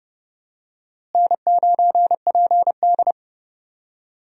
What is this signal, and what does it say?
Morse code, heard as a keyed tone, about 32 wpm, 700 Hz. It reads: N9PD
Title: USA Call Signs - 4 Characters - Repeated 3x - 30wpm